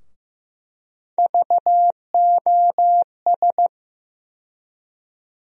Morse code, heard as a keyed tone, about 15 wpm, 700 Hz. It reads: VOS
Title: USA Call Sign Postfixes - 3 Letters 15wpm